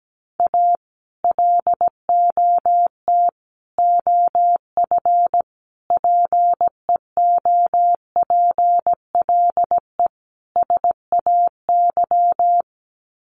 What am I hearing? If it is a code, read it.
Morse code, heard as a keyed tone, about 17 wpm, 700 Hz. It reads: A LOT OF PEOPLE SAY
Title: Sets of 5 Words Encore2 17wpm